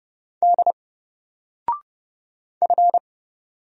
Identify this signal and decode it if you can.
Morse code, heard as a keyed tone, about 31 wpm, 700 Hz. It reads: D F
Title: Single Letter-Number 30wpm